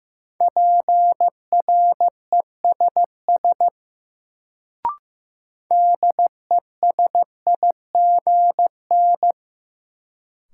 Morse code, heard as a keyed tone, about 15 wpm, 700 Hz. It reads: PRESS DESIGN
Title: Sentences from Top 800 Words - New Word Drill - 15wpm